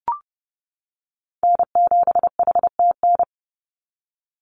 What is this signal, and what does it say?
Morse code, heard as a keyed tone, about 32 wpm, 700 Hz. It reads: N7HTN
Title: USA Call Signs 30wpm